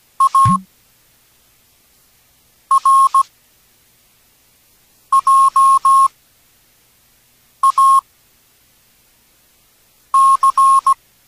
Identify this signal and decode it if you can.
Morse code, heard as a keyed tone, about 17 wpm, spaced as for 4 wpm, 1080 Hz. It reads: ARJAC
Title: Proyecto Cancun, Infraestructura Experimental de Texto a Voz para Radioaficionados